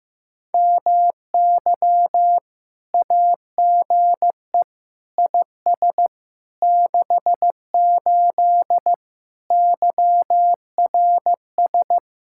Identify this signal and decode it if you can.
Morse code, heard as a keyed tone, about 15 wpm, 700 Hz. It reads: MY AGE IS 68 YRS
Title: Speed-Racing Common QSO Phrases Encore - 15wpm